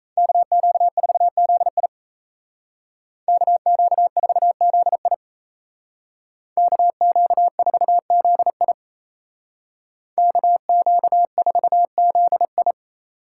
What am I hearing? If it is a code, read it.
Morse code, heard as a keyed tone, about 33 wpm, 700 Hz. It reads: KQ4ZI KQ4ZI KQ4ZI KQ4ZI
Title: Speed Racing - USA 2x2 Call Signs 28wpm